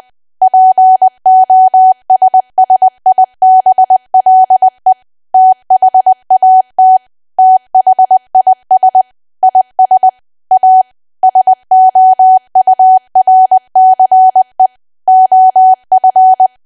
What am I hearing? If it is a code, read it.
Morse code, heard as a keyed tone, about 20 wpm, 750 Hz. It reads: POSSIBLE THAT THIS IS A SOURCE OF
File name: cwa / W1AW-Level3 / 20WPM